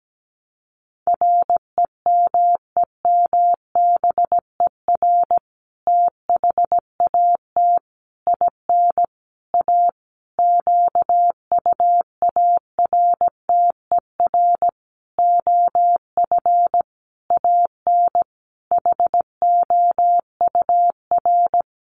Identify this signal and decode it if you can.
Morse code, heard as a keyed tone, about 17 wpm, 700 Hz. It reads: REMEMBER THAT IN A QUARTER OF AN HOUR
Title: Sentences from Top 1300 Words 17wpm